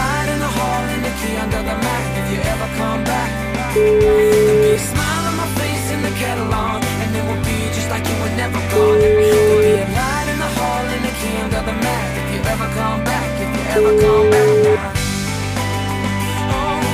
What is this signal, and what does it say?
Ringback tone (Europe, 425 Hz), with noise.